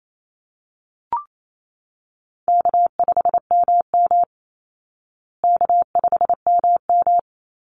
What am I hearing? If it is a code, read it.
Morse code, heard as a keyed tone, about 28 wpm, 700 Hz. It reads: K5MM K5MM
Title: Call Signs Repeated 2x - 28wpm